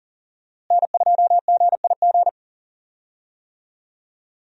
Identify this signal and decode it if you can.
Morse code, heard as a keyed tone, about 38 wpm, 690 Hz. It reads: N2GIG
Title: USA Call Signs - 5 Characters - Repeated 3x - 40wpm